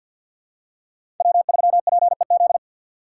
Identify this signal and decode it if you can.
Morse code, heard as a keyed tone, about 49 wpm, 700 Hz. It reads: W3PEZ
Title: USA Call Signs - 3x Character Spacing - 50wpm